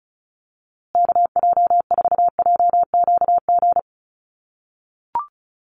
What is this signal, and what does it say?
Morse code, heard as a keyed tone, about 34 wpm, 700 Hz. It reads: KJ4JQG
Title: USA Call Signs Repeated 3x 35wpm